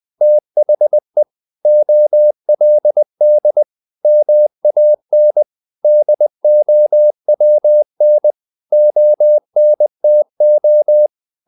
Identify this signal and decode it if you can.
Morse code, heard as a keyed tone, about 20 wpm, 600 Hz. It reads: THE OLD MAN DOWN ONTO
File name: the old man and the sea / the old man and the sea part 1 20wpm